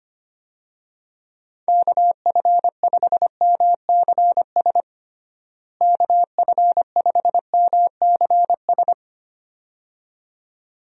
Morse code, heard as a keyed tone, about 25 wpm, 700 Hz. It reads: KF5MCS KF5MCS
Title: USA Call Signs - 6 Characters - Repeated 2x - 25wpm